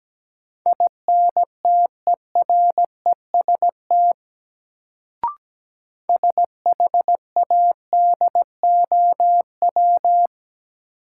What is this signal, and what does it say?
Morse code, heard as a keyed tone, about 17 wpm, 700 Hz. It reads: INTEREST SHADOW
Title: Sets of 5 Words Encore - New Word Drill - 17wpm